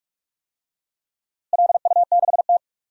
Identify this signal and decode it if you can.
Morse code, heard as a keyed tone, about 47 wpm, 700 Hz. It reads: LU6T
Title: DXCC-100 Call Signs - 4 Characters - Repeated 2x - 45wpm